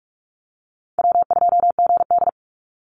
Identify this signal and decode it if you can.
Morse code, heard as a keyed tone, about 44 wpm, 700 Hz. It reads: W2GD
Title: Easy Call Signs Repeated 2x - 45wpm